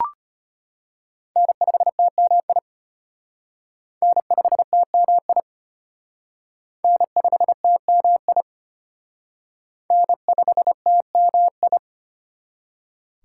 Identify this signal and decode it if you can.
Morse code, heard as a keyed tone, about 32 wpm, 700 Hz. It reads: N5TMI N5TMI N5TMI N5TMI